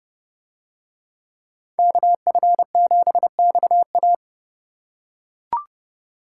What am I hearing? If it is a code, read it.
Morse code, heard as a keyed tone, about 28 wpm, 700 Hz. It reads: KF7XA